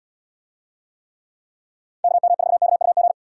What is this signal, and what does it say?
Morse code, heard as a keyed tone, about 78 wpm, 700 Hz. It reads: KM4KUG